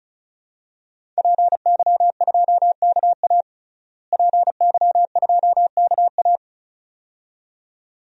Morse code, heard as a keyed tone, about 36 wpm, 700 Hz. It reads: PY2KA PY2KA
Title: DXCC-100 Call Signs - 5 Characters - Repeated 2x - 35wpm